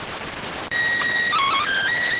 SSTV (slow-scan television), PD 290.